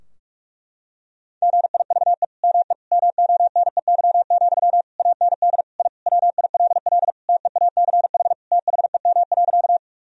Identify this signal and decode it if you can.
Morse code, heard as a keyed tone, about 46 wpm, 700 Hz. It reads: GIVE ME MONEY, AND I WILL TEACH THEM.
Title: Sentences from Top 1000 Words - Review Part 2 - 45wpm